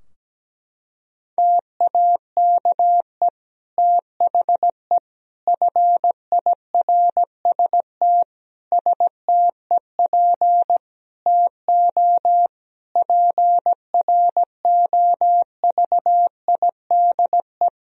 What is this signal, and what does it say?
Morse code, heard as a keyed tone, about 17 wpm, 700 Hz. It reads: TAKE THE FIRST STEP TO PROVIDE
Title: Sentences from Top 1000 Words - Review Part 4 - 17wpm